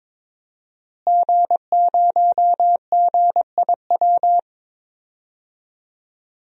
Morse code, heard as a keyed tone, about 22 wpm, 700 Hz. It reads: G0GIW
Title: DXCC-100 Call Signs - 5 Characters - Repeated 3x - 22wpm